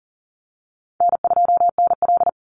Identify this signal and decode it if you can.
Morse code, heard as a keyed tone, about 40 wpm, 700 Hz. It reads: N2NL
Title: Easy Call Signs Repeated 2x - 40wpm